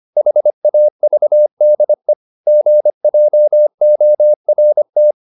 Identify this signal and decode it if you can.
Morse code, heard as a keyed tone, about 25 wpm, 600 Hz. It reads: HAVDE GJORT
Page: Ferie 25 wpm